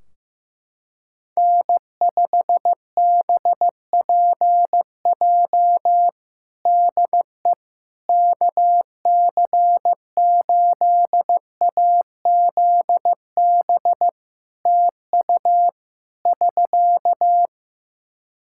Morse code, heard as a keyed tone, about 15 wpm, 700 Hz. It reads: N5BPJ DE KC8AZB TU <SK>
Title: K1USN SST QSO Practice 15wpm